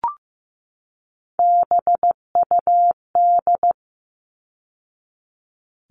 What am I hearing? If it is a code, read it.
Morse code, heard as a keyed tone, about 15 wpm, 700 Hz. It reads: BUD
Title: All Cut Numbers - 3-Digit 15wpm